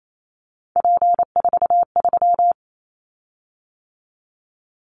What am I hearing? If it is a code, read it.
Morse code, heard as a keyed tone, about 28 wpm, 700 Hz. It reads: P43
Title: Callsign Prefixes for Top 100 DXCC - 28wpm